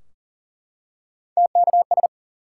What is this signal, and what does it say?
Morse code, heard as a keyed tone, about 39 wpm, 700 Hz. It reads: TKS
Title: Top 180 QSO Elements 40wpm